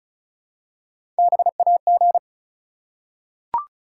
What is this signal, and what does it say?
Morse code, heard as a keyed tone, about 36 wpm, 700 Hz. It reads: BAG
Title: Three Letter Words - 35wpm